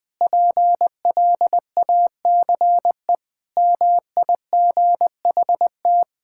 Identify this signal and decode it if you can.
Morse code, heard as a keyed tone, about 20 wpm, 700 Hz. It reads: PLACE MIGHT
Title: Sentences from Top 1600 Words 20wpm